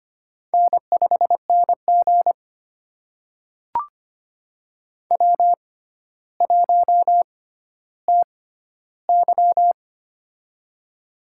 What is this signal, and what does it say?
Morse code, heard as a keyed tone, about 25 wpm, spaced as for 13 wpm, 700 Hz. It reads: N5NG W1TY
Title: USA Call Signs - 4 Characters - 6x Character Spacing - 25wpm